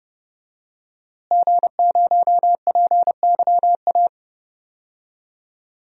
Morse code, heard as a keyed tone, about 30 wpm, 700 Hz. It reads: G0PYA
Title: DXCC-100 Call Signs - 5 Characters - Repeated 3x - 30wpm